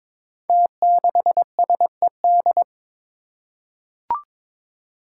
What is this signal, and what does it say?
Morse code, heard as a keyed tone, about 22 wpm, 700 Hz. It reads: T6SED
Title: Five Challenging Letter-Number Combinations 22wpm